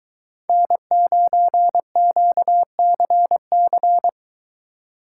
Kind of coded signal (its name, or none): Morse code